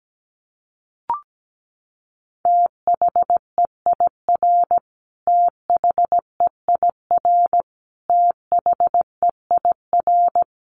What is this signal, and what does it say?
Morse code, heard as a keyed tone, about 17 wpm, 700 Hz. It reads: THEIR THEIR THEIR
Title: Top 500 Words Repeated 17wpm